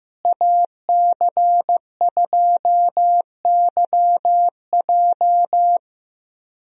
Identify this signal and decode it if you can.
Morse code, heard as a keyed tone, about 15 wpm, 700 Hz. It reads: AC2YJ
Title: USA Call Signs - 5 Characters - Repeated 2x - 15wpm